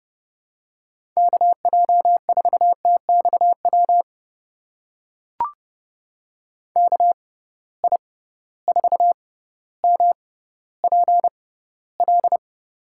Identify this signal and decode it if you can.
Morse code, heard as a keyed tone, about 31 wpm, spaced as for 15 wpm, 700 Hz. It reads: KJ4TXW KI4MPL
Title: USA Call Signs - 6 Characters - 6x Character Spacing - 30wpm